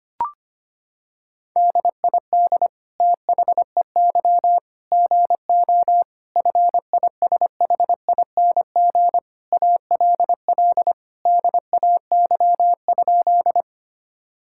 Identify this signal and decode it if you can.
Morse code, heard as a keyed tone, about 25 wpm, 700 Hz. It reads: DID THEY GO FISHING ALL DAY?
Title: Sentences from Top 1000 Words - Review Part 5 - 25wpm